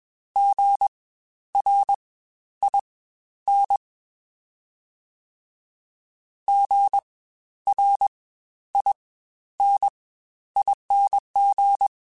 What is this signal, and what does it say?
Morse code, heard as a keyed tone, about 21 wpm, spaced as for 11 wpm, 800 Hz. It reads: GRIN GRINING
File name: Spaced Alphas